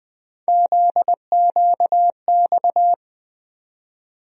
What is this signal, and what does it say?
Morse code, heard as a keyed tone, about 20 wpm, 700 Hz. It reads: ZQX
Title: USA Call Sign Postfixes - 3 Letters 20wpm